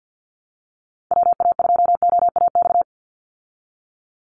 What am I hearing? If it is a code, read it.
Morse code, heard as a keyed tone, about 50 wpm, 700 Hz. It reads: WA2OAX
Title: Call Signs Repeated 2x - 50wpm